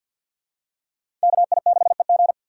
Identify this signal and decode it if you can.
Morse code, heard as a keyed tone, about 53 wpm, 700 Hz. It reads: KI6EG
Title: USA Call Signs - 5 Characters - 4x Character Spacing - 50wpm